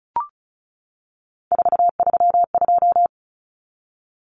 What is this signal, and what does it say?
Morse code, heard as a keyed tone, about 34 wpm, 700 Hz. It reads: 432
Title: Common Cut Numbers - 3-Digit 35wpm